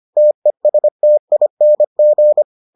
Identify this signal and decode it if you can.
Morse code, heard as a keyed tone, about 25 wpm, 600 Hz. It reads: TESTING